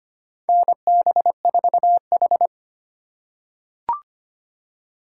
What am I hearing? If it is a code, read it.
Morse code, heard as a keyed tone, about 25 wpm, 700 Hz. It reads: NB4H
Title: USA Call Signs - 4 Characters - 3x Character Spacing - 25wpm